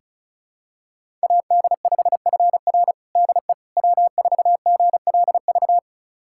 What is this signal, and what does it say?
Morse code, heard as a keyed tone, about 36 wpm, 700 Hz. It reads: AD5FR DE W4GLV
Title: USA Callsign DE Callsign - 5 Characters - 35wpm